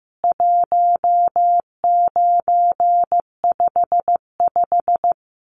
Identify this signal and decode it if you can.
Morse code, heard as a keyed tone, about 15 wpm, 700 Hz. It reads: 1955
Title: CWT Exchange 15wpm